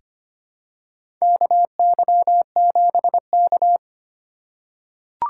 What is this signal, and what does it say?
Morse code, heard as a keyed tone, about 25 wpm, 700 Hz. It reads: KY7K
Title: K1USN SST QSO Practice - 25wpm